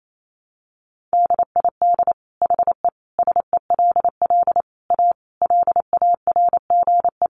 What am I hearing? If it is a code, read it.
Morse code, heard as a keyed tone, about 28 wpm, 700 Hz. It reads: DID HE SELL A LARGE
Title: Sentences from Top 1300 Words 28wpm